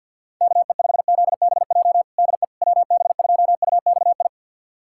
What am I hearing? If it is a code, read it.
Morse code, heard as a keyed tone, about 50 wpm, 700 Hz. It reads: KE5ZBJ DE WB2UXI
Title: USA Callsign DE Callsign 50wpm